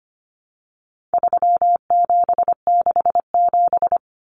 Morse code, heard as a keyed tone, about 25 wpm, 700 Hz. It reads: 3767
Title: Numbers - 4-Digit 25wpm